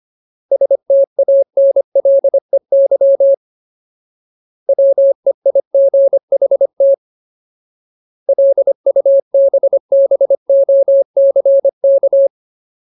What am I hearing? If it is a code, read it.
Morse code, heard as a keyed tone, about 25 wpm, 550 Hz. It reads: STANLEY WEIGHT LUBBOCK
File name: Ultimate Word List / 25WPM Ultimate